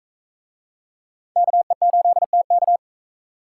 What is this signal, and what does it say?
Morse code, heard as a keyed tone, about 40 wpm, 700 Hz. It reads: KE8TK